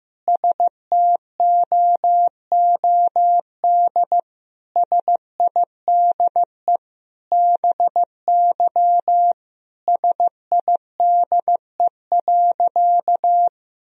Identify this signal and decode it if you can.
Morse code, heard as a keyed tone, about 15 wpm, 700 Hz. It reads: STOOD SIDE BY SIDE.
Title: Sentences from Top 1000 Words - Review Part 2 - 15wpm